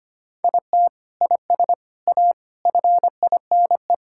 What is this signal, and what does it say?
Morse code, heard as a keyed tone, about 25 wpm, 700 Hz. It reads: IT IS A FINE
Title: Sentences from Top 1600 Words 25wpm